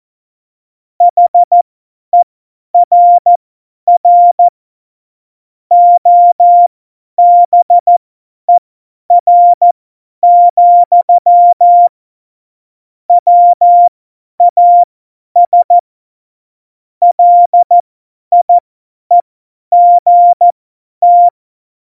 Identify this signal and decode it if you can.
Morse code, heard as a keyed tone, about 14 wpm, spaced as for 10 wpm, 700 Hz. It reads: HERR OBER, WAS LIEGT
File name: Morsekurs / Witze / Witz 02. Juli 2019, 50 F70